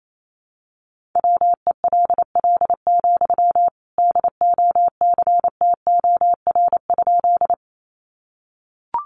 Morse code, heard as a keyed tone, about 28 wpm, 700 Hz. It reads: WELL, DOCTOR?
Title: Sentences from Top 1500 Words 28wpm